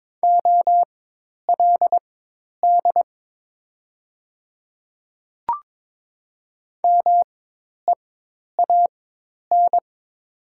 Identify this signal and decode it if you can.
Morse code, heard as a keyed tone, about 22 wpm, spaced as for 10 wpm, 700 Hz. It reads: OLD MEAN
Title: 4x Character Spacing - Top 300 Words - 22wpm